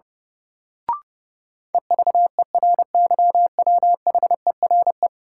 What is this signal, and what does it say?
Morse code, heard as a keyed tone, about 30 wpm, 700 Hz. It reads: EVERYWHERE
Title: Sentences from Top 1600 Words - New Word Drill - 30wpm